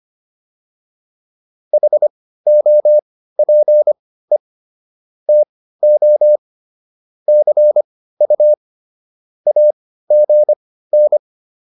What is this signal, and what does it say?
Morse code, heard as a keyed tone, about 25 wpm, spaced as for 15 wpm, 600 Hz. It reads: HOPE TO CU AGN